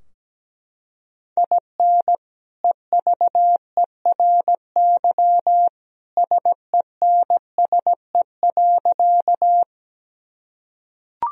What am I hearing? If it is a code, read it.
Morse code, heard as a keyed tone, about 17 wpm, 700 Hz. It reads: IN EVERY SENSE.
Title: Sentences from Top 1000 Words - Review Part 4 - 17wpm